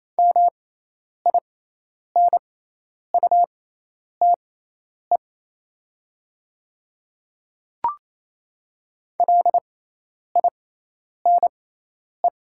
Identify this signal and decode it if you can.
Morse code, heard as a keyed tone, about 28 wpm, spaced as for 10 wpm, 700 Hz. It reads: MINUTE LINE
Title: 6x Character Spacing - Top 500 Words - 28wpm